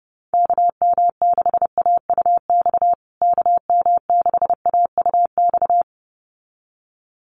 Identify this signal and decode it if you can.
Morse code, heard as a keyed tone, about 30 wpm, 700 Hz. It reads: KM6AUX KM6AUX
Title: USA Call Signs Repeated 2x 30wpm